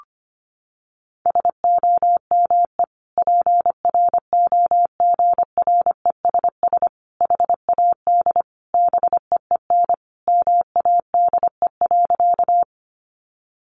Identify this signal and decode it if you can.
Morse code, heard as a keyed tone, about 25 wpm, 700 Hz. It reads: SOME PROGRESS HAD BEEN MADE.
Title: Sentences from Top 1400 Words 25wpm